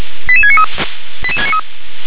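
Five-tone selective call (ZVEI), heard as quick runs of five tones, tones of about 70 ms. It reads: 88693
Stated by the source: Five-tone ZVEI selective call received on an SDR.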